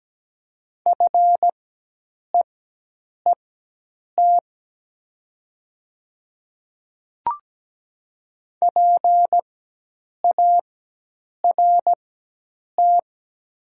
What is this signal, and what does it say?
Morse code, heard as a keyed tone, about 17 wpm, spaced as for 8 wpm, 700 Hz. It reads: FEET PART
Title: 4x Character Spacing - Top 500 Words - 17wpm